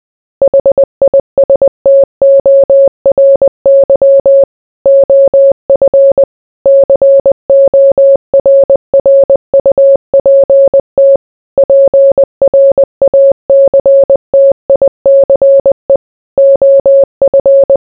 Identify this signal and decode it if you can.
Morse code, heard as a keyed tone, about 20 wpm, 550 Hz. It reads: HISTORY OF CORRUPT PRACTICE OF